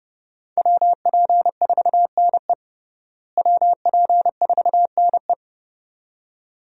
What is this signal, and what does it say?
Morse code, heard as a keyed tone, about 32 wpm, 700 Hz. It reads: WP4NE WP4NE
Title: USA Call Signs - 5 Characters - Repeated 2x - 30wpm